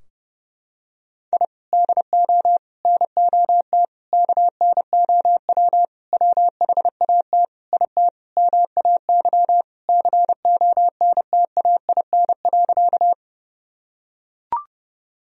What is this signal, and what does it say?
Morse code, heard as a keyed tone, about 30 wpm, 700 Hz. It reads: I DO NOT KNOW WHAT IT MAY CONTAIN.
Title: Sentences from Top 1000 Words - Review Part 2 - 30wpm